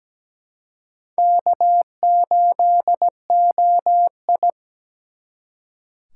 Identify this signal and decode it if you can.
Morse code, heard as a keyed tone, about 20 wpm, 700 Hz. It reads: K8OI
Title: Speed Racing - USA Call Signs 17wpm